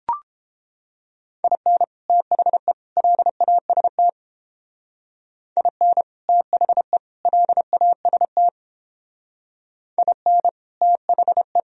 Morse code, heard as a keyed tone, about 32 wpm, 700 Hz. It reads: IN THE LAST IN THE LAST IN THE